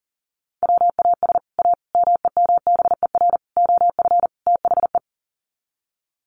Morse code, heard as a keyed tone, about 40 wpm, 700 Hz. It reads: WAS A MEMBER OF THE
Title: Sets of 5 Words 40wpm